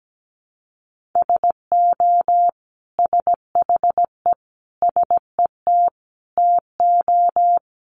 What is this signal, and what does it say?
Morse code, heard as a keyed tone, about 17 wpm, 700 Hz. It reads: SO SHE SET TO
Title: Sentences from Top 900 Words 17wpm